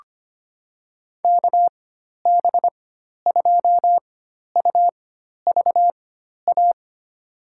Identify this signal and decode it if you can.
Morse code, heard as a keyed tone, about 25 wpm, spaced as for 12 wpm, 700 Hz. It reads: KB2UVA
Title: USA Call Signs - 6 Characters - 4x Character Spacing - 25wpm